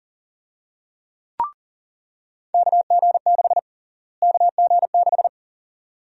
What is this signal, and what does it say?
Morse code, heard as a keyed tone, about 40 wpm, 700 Hz. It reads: KG6 KG6
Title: USA Call Sign Prefixes - Repeated 2x - 40wpm